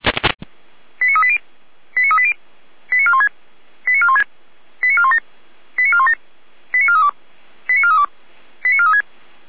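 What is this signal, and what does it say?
Five-tone selective call (ZVEI), heard as quick runs of five tones, tones of about 70 ms. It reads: 89380 89380 89416 89416 89417 89417 89432 89432 89436
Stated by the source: ZVEI five-tone selective call 89401 received on SDR.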